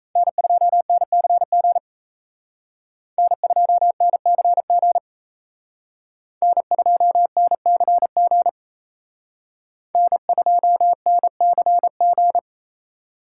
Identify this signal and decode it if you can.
Morse code, heard as a keyed tone, about 35 wpm, 700 Hz. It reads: N2NCG N2NCG N2NCG N2NCG
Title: Speed Racing - USA 1x3 Call Signs 28wpm